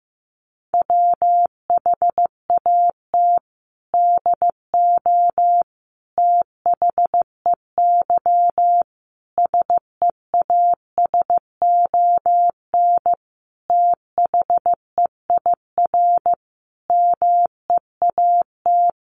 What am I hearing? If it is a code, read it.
Morse code, heard as a keyed tone, about 15 wpm, 700 Hz. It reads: WHAT DO THEY SEASON THEIR MEAT
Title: Sentences from Top 1400 Words 15wpm